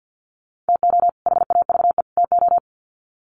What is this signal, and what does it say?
Morse code, heard as a keyed tone, about 56 wpm, 700 Hz. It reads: TO HAVE TO
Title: Sets of 3 Words 50wpm